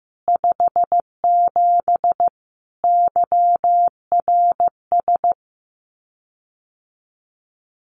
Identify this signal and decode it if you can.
Morse code, heard as a keyed tone, about 14 wpm, 700 Hz. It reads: 57 YRS
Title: Common QSO Phrases Encore 15wpm